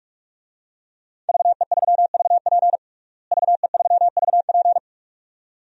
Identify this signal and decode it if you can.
Morse code, heard as a keyed tone, about 46 wpm, 700 Hz. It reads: VE3VP VE3VP
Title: DXCC-100 Call Signs - 5 Characters - Repeated 2x - 45wpm